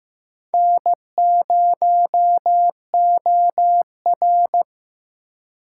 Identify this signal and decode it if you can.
Morse code, heard as a keyed tone, about 15 wpm, 700 Hz. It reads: N0OR
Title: USA Call Signs - 2x Character Spacing - 15wpm